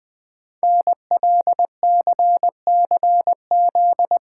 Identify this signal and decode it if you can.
Morse code, heard as a keyed tone, about 20 wpm, 700 Hz. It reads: NLCCZ
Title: Code Groups - 5 Letters - 20wpm